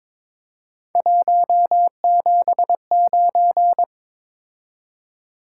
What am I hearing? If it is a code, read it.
Morse code, heard as a keyed tone, about 22 wpm, 700 Hz. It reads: 179
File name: practice / 22wpm / Three Letter-Number Combinations - 22wpm